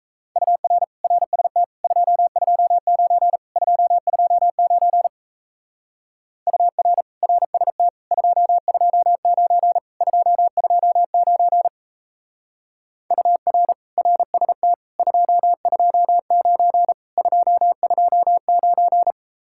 Morse code, heard as a keyed tone, about 33 wpm, 700 Hz. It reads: UR RST 229 229 UR RST 229 229 UR RST 229 229
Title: Speed-Racing Common QSO Phrases - 28wpm